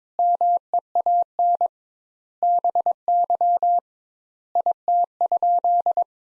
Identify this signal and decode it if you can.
Morse code, heard as a keyed tone, about 22 wpm, 700 Hz. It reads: MEAN BY IT?
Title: Sentences Top 200 Words - 22wpm